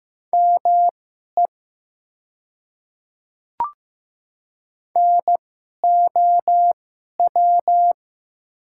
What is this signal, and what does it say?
Morse code, heard as a keyed tone, about 15 wpm, spaced as for 11 wpm, 700 Hz. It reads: ME NOW
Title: 2x Character Spacing - Top 200 Words - 15wpm